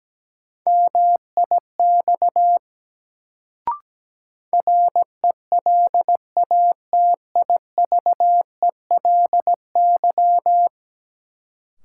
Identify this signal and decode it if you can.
Morse code, heard as a keyed tone, about 17 wpm, 700 Hz. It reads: MIX RELATIVELY
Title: Sentences from Top 1100 Words - New Word Drill Alt - 17wpm